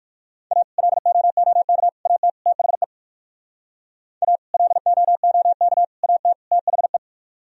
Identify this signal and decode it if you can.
Morse code, heard as a keyed tone, about 64 wpm, spaced as for 56 wpm, 700 Hz. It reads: A LOOK AT THE A LOOK AT THE
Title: Speed-Racing - Sets of 4 Words 45wpm